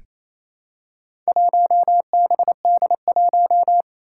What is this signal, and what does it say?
Morse code, heard as a keyed tone, about 28 wpm, 700 Hz. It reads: 1BD1
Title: Four Challenging Letter-Number Combinations 28wpm